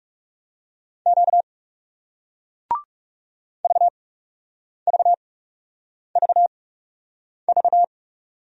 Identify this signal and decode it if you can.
Morse code, heard as a keyed tone, about 40 wpm, spaced as for 8 wpm, 700 Hz. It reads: Q VVVV